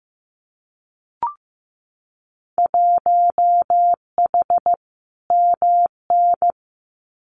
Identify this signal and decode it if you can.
Morse code, heard as a keyed tone, about 15 wpm, 700 Hz. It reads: 1H MN
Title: Winter Field Day Exchanges 15wpm